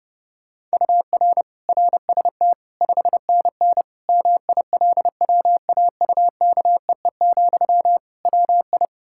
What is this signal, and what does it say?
Morse code, heard as a keyed tone, about 30 wpm, 700 Hz. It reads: UR RST 5NN MILWAUKEE, WI
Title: K1USN SST QSO Practice 30wpm